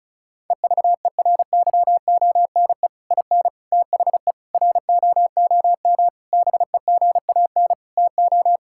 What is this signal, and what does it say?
Morse code, heard as a keyed tone, about 36 wpm, 700 Hz. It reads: EVERYONE IN THE ROOM BEGAN TO
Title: Sentences from Top 1000 Words - Review Part 5 - 35wpm